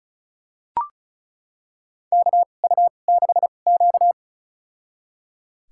Morse code, heard as a keyed tone, about 36 wpm, spaced as for 25 wpm, 700 Hz. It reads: KU6Q
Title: USA Call Signs - 4 Characters - 2x Character Spacing - 35wpm